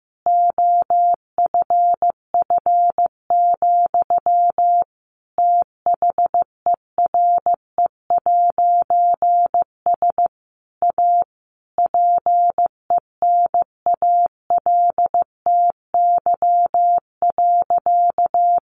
Morse code, heard as a keyed tone, about 15 wpm, 700 Hz. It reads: OFF, THERE'S A PENALTY.
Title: A Sound Of Thunder 15wpm